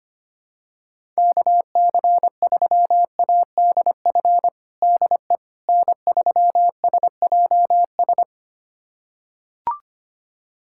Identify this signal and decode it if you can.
Morse code, heard as a keyed tone, about 25 wpm, 700 Hz. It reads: KC3ADF DE N3SJS
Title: USA Callsign DE Callsign 25wpm